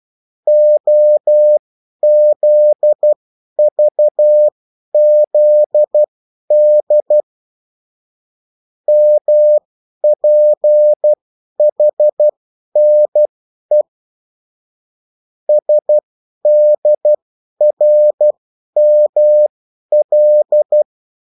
Morse code, heard as a keyed tone, about 12 wpm, spaced as for 10 wpm, 600 Hz. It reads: OZVZD MPHNE SDRML
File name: morse mp3 / revision9-vitesse 10 mots